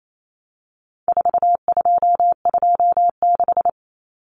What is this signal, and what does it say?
Morse code, heard as a keyed tone, about 28 wpm, 700 Hz. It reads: 4226
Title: Common Cut Numbers - 4-Digit 28wpm